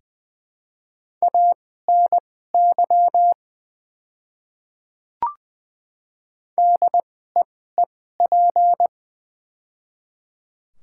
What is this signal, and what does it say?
Morse code, heard as a keyed tone, about 20 wpm, spaced as for 14 wpm, 700 Hz. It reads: ANY DEEP
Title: 2x Character Spacing - Top 500 Words - 20wpm